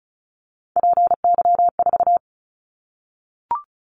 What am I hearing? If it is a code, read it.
Morse code, heard as a keyed tone, about 36 wpm, 700 Hz. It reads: PY4
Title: Callsign Prefixes for Top 100 DXCC - 35wpm